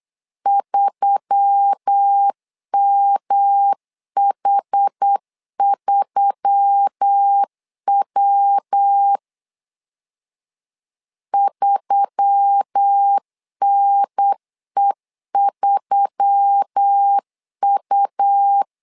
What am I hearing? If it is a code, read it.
Morse code, heard as a keyed tone, about 8 wpm, 800 Hz. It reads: 3MH3W 3NE3U
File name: LEZIONI / 4 Q1S3 8wpm 3037